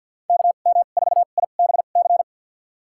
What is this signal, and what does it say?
Morse code, heard as a keyed tone, about 52 wpm, spaced as for 36 wpm, 700 Hz. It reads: KM4IBC
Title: USA Call Signs - 2x Character Spacing - 50wpm